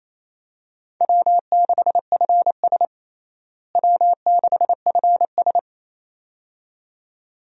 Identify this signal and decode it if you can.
Morse code, heard as a keyed tone, about 27 wpm, 700 Hz. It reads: W6FS W6FS
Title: USA Call Signs - 4 Characters - Repeated 2x - 28wpm